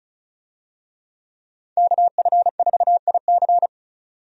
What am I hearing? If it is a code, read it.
Morse code, heard as a keyed tone, about 36 wpm, 700 Hz. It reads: KF4IC
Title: USA Call Signs - 5 Characters - 35wpm